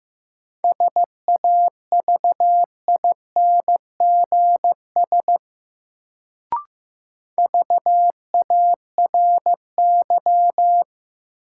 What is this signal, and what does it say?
Morse code, heard as a keyed tone, about 15 wpm, 700 Hz. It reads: SAVINGS VARY
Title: Sentences from Top 1200 Words - New Word Drill - 15wpm